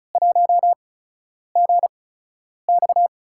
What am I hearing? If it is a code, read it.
Morse code, heard as a keyed tone, about 36 wpm, spaced as for 10 wpm, 700 Hz. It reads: 1GX
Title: USA Call Signs - 4 Characters - 8x Character Spacing - 35wpm